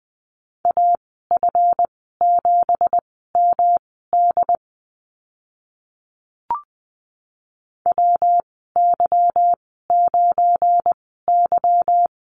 Morse code, heard as a keyed tone, about 20 wpm, spaced as for 14 wpm, 700 Hz. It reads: AF7MD WY9Y
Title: 200 Lesson 32g - Letter k - Call Signs 20wpm